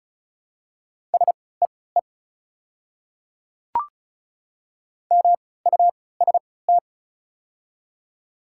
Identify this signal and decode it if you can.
Morse code, heard as a keyed tone, about 36 wpm, spaced as for 20 wpm, 700 Hz. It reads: SEE MUST
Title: 3x Character Spacing - Top 300 Words - 35wpm